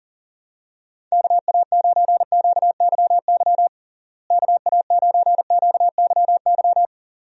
Morse code, heard as a keyed tone, about 40 wpm, 700 Hz. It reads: KA9QYY KA9QYY
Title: USA Call Signs - 6 Characters - Repeated 2x - 40wpm